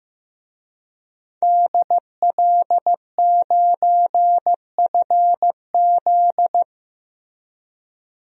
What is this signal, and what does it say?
Morse code, heard as a keyed tone, about 15 wpm, 700 Hz. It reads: DL9FZ
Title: DXCC-100 Call Signs - 5 Characters - Repeated 3x - 15wpm